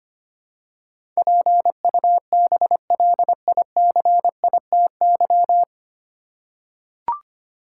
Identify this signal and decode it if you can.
Morse code, heard as a keyed tone, about 25 wpm, 700 Hz. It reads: PUBLICITY
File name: practice / 25wpm / 9 Letter Words - First 3 Letters Given - 25wpm